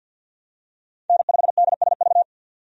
Morse code, heard as a keyed tone, about 53 wpm, 700 Hz. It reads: N5DSV
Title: USA Call Signs - 5 Characters - 50wpm